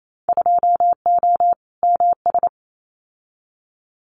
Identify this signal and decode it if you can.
Morse code, heard as a keyed tone, about 28 wpm, 700 Hz. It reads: 2O MS
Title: Winter Field Day Exchanges 28wpm